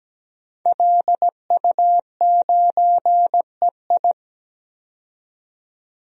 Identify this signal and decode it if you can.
Morse code, heard as a keyed tone, about 17 wpm, 700 Hz. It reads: LU9EI
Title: DXCC-100 Call Signs - 5 Characters - 17wpm